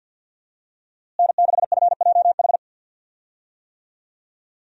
Morse code, heard as a keyed tone, about 52 wpm, 700 Hz. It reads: N6FJH
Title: USA Call Signs - 5 Characters - Repeated 3x - 50wpm